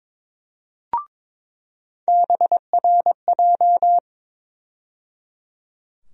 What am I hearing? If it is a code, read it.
Morse code, heard as a keyed tone, about 22 wpm, 700 Hz. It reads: BRJ
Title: USA Call Sign Postfixes - 3 Letters 22wpm